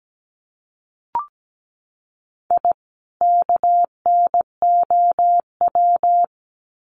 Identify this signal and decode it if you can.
Morse code, heard as a keyed tone, about 17 wpm, 700 Hz. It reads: I KNOW
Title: Sets of 2 Words 17wpm